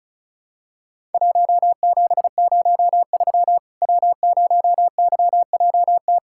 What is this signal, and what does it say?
Morse code, heard as a keyed tone, about 36 wpm, 700 Hz. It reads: 1703 W0YJT
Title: Worked All Europe DX Contest - QTC Exchange - 35wpm